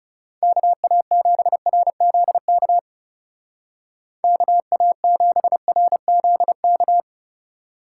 Morse code, heard as a keyed tone, about 36 wpm, 700 Hz. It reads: KA7RZK KA7RZK
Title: Speed Racing - USA 2x3 Call Signs 30wpm